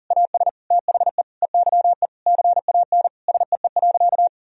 Morse code, heard as a keyed tone, about 37 wpm, 700 Hz. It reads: AS THE EYE CAN SEE.
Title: Sentences from Top 1000 Words - Review Part 1 - 40wpm